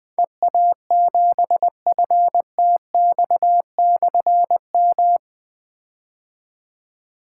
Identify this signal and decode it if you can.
Morse code, heard as a keyed tone, about 20 wpm, 700 Hz. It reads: EA7FTX/M
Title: Difficult Call Signs 20wpm